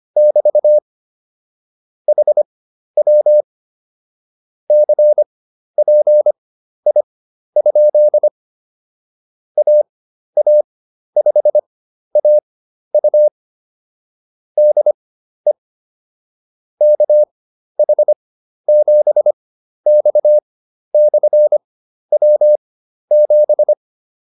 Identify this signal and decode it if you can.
Morse code, heard as a keyed tone, about 25 wpm, spaced as for 12 wpm, 600 Hz. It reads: <BT> HW CPI? AA5AU DE KH7X/W7